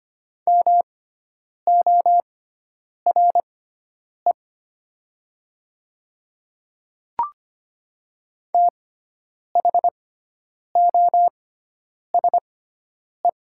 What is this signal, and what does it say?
Morse code, heard as a keyed tone, about 25 wpm, spaced as for 9 wpm, 700 Hz. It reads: MORE THOSE
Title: 6x Character Spacing - Top 300 Words - 25wpm